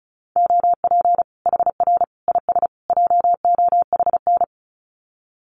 Morse code, heard as a keyed tone, about 34 wpm, 700 Hz. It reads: OP HR IS JOHN
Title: Common QSO Phrases 35wpm